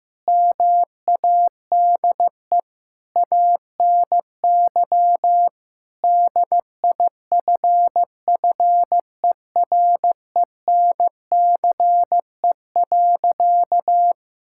Morse code, heard as a keyed tone, about 16 wpm, 700 Hz. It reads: MADE ANY DIFFERENCE.
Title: Sentences from Top 1000 Words - Review Part 5 - 15wpm